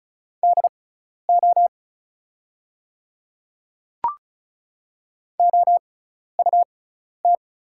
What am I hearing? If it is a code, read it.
Morse code, heard as a keyed tone, about 36 wpm, spaced as for 12 wpm, 700 Hz. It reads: DO OUT
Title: 6x Character Spacing - Top 100 Words - 35wpm